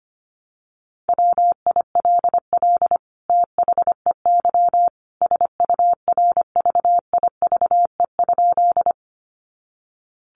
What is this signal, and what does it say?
Morse code, heard as a keyed tone, about 25 wpm, 700 Hz. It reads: WILL THEY SURVIVE?
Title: Sentences from Top 1300 Words 25wpm